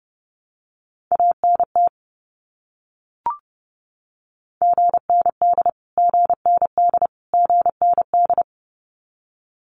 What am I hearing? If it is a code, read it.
Morse code, heard as a keyed tone, about 30 wpm, 700 Hz. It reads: ANT GND GND GND
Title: Abbreviations Repeated 3x - 30wpm